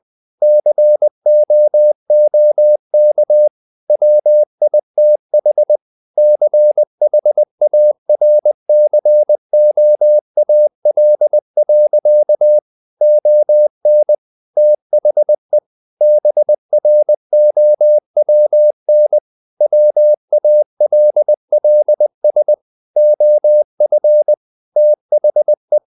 Morse code, heard as a keyed tone, about 20 wpm, 600 Hz. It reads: COOK WITH CHARCOAL. ON THE BROWN WALLS OF THE
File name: the old man and the sea / the old man and the sea part 1 20wpm